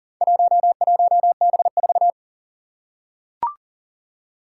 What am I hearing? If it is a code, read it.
Morse code, heard as a keyed tone, about 40 wpm, 700 Hz. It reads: 11B4
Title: Four Challenging Letter-Number Combinations 40wpm